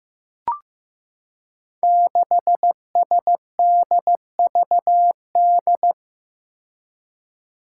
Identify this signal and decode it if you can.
Morse code, heard as a keyed tone, about 15 wpm, 700 Hz. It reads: 6SDVD
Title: Five Challenging Letter-Number Combinations 15wpm